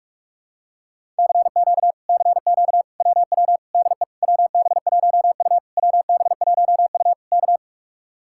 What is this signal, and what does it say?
Morse code, heard as a keyed tone, about 49 wpm, 700 Hz. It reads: CQ CQ WW DE WB1U WB1U K